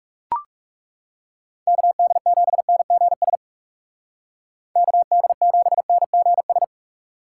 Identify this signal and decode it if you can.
Morse code, heard as a keyed tone, about 39 wpm, 700 Hz. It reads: KD7NGS KD7NGS